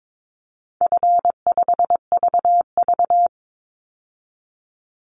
Morse code, heard as a keyed tone, about 22 wpm, 700 Hz. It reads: F5VV
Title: Easy Call Signs Repeated 2x - 22wpm